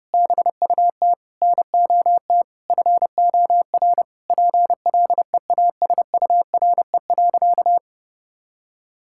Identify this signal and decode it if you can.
Morse code, heard as a keyed tone, about 30 wpm, 700 Hz. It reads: BUT NOT FOR PLEASURE.
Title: Sentences from Top 1600 Words 30wpm